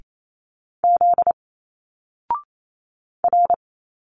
Morse code, heard as a keyed tone, about 28 wpm, 700 Hz. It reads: Z R
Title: Single Letters 28wpm